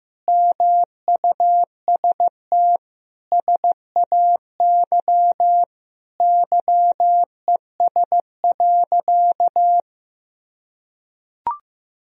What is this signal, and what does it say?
Morse code, heard as a keyed tone, about 15 wpm, 700 Hz. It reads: MUST SAY YES.
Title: Sentences from Top 500 Words and Abbreviations 15wpm